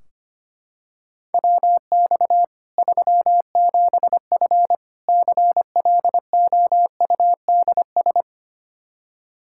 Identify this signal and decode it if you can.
Morse code, heard as a keyed tone, about 25 wpm, 700 Hz. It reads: WX 37F CLOUDS
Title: Practice QSO - Exchange 2 - Info - 25wpm